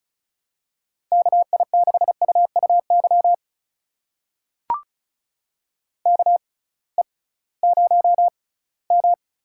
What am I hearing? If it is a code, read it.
Morse code, heard as a keyed tone, about 36 wpm, spaced as for 21 wpm, 700 Hz. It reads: KI6UUY KE0M
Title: USA Call Signs - 6 Characters - 6x Character Spacing - 35wpm